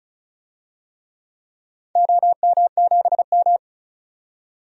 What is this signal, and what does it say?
Morse code, heard as a keyed tone, about 34 wpm, 700 Hz. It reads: OM7M
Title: DXCC-100 Call Signs - 4 Characters - 35wpm